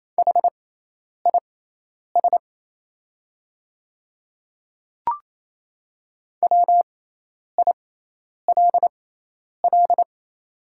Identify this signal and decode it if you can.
Morse code, heard as a keyed tone, about 28 wpm, spaced as for 10 wpm, 700 Hz. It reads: HIS WILL
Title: 6x Character Spacing - Top 500 Words - 28wpm